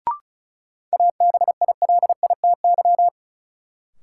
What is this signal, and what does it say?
Morse code, heard as a keyed tone, about 36 wpm, 700 Hz. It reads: ABILITY
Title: Sentences from Top 1000 Words - Review Part 3 - New Word Drill - 35wpm